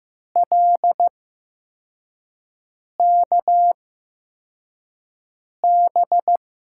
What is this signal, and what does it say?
Morse code, heard as a keyed tone, about 15 wpm, spaced as for 4 wpm, 700 Hz. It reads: LKB
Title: USA Call Signs - 8x Character Spacing - 15wpm